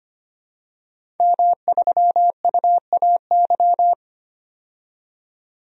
Morse code, heard as a keyed tone, about 25 wpm, 700 Hz. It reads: M3UAY
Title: DXCC-100 Call Signs - 5 Characters - 25wpm